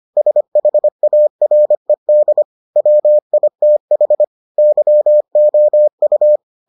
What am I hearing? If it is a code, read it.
Morse code, heard as a keyed tone, about 25 wpm, 600 Hz. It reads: SHARED WITH YOU